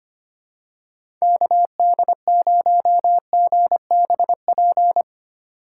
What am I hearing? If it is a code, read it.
Morse code, heard as a keyed tone, about 25 wpm, 690 Hz. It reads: KD0GBP
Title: USA Call Signs - 6 Characters - 3x Character Spacing - 25wpm